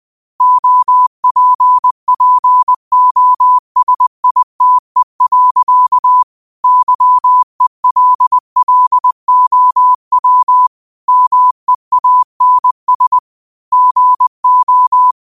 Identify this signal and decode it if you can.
Morse code, heard as a keyed tone, about 20 wpm, 1000 Hz. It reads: OPPOSITE. YELLOW MEANS GO
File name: qotdfiles / 20wpm